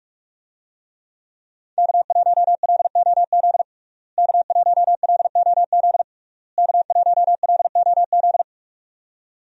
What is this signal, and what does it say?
Morse code, heard as a keyed tone, about 45 wpm, 700 Hz. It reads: K1LOZ K1LOZ K1LOZ
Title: USA Call Signs - 5 Characters - Repeated 3x - 45wpm